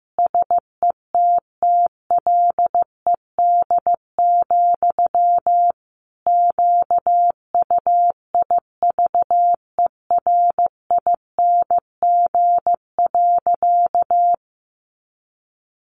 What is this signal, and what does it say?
Morse code, heard as a keyed tone, about 15 wpm, 700 Hz. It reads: SETTLED, QUIVERING.
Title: A Sound Of Thunder 15wpm